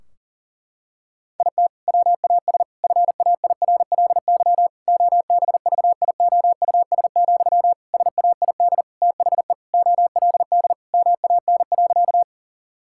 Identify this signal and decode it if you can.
Morse code, heard as a keyed tone, about 40 wpm, 700 Hz. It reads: IT WAS FAIRLY OBVIOUS, SAID THE OLD MAN.
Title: Sentences from Top 1600 Words 40wpm